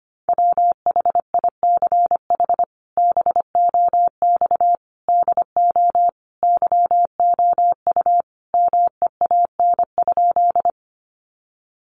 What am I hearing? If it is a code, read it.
Morse code, heard as a keyed tone, about 25 wpm, 700 Hz. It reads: WHICH BOX DO YOU MEAN?
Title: Sentences from Top 500 Words 25wpm